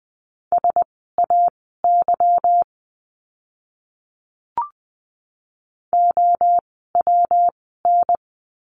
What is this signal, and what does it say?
Morse code, heard as a keyed tone, about 20 wpm, spaced as for 15 wpm, 700 Hz. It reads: SAY OWN
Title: 237 Lesson 38f - Letter q - Words 20wpm